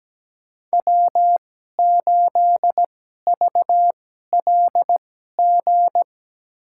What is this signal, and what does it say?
Morse code, heard as a keyed tone, about 17 wpm, spaced as for 12 wpm, 700 Hz. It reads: W8VLG
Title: USA Call Signs - 5 Characters - 2x Character Spacing - 17wpm